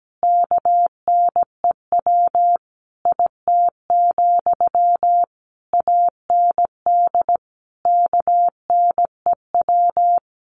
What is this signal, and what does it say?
Morse code, heard as a keyed tone, about 17 wpm, 700 Hz. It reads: KNEW IT, AND KNEW
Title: Sentences from Top 1200 Words 17wpm